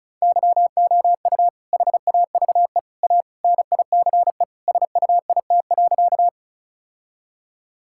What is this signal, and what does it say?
Morse code, heard as a keyed tone, about 36 wpm, 700 Hz. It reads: YOU HAVE A NICE SUIT.
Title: Sentences from Top 1600 Words 35wpm